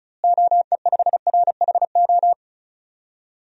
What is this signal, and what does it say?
Morse code, heard as a keyed tone, about 36 wpm, 700 Hz. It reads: OE5RHO
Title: DXCC-100 Call Signs - 6 Characters - Repeated 3x - 35wpm